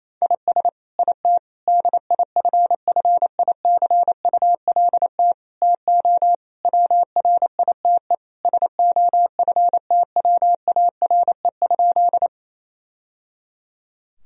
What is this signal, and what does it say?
Morse code, heard as a keyed tone, about 28 wpm, 700 Hz. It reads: IS IT DIFFICULT TO WRITE SOFTWARE?
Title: Sentences from Top 1000 Words - Review Part 3 - 28wpm